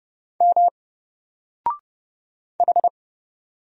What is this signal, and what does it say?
Morse code, heard as a keyed tone, about 30 wpm, 700 Hz. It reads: M H
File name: Single-Letter-Number-30wpm